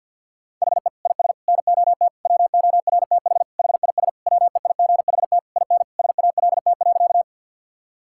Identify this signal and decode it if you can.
Morse code, heard as a keyed tone, about 50 wpm, 700 Hz. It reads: HE IS NOT WORTH HIS WEIGHT IN SALT.
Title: Sentences from Top 1000 Words - Review Part 5 - 50wpm